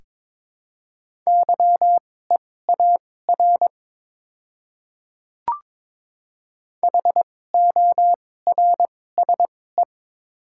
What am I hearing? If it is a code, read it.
Morse code, heard as a keyed tone, about 22 wpm, spaced as for 16 wpm, 690 Hz. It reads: YEAR HORSE
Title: 2x Character Spacing - Top 300 Words - 22wpm